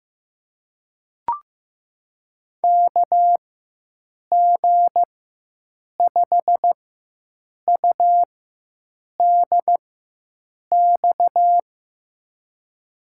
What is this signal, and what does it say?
Morse code, heard as a keyed tone, about 15 wpm, spaced as for 7 wpm, 700 Hz. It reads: KG5UDX
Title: USA Call Signs - 4x Character Spacing - 15wpm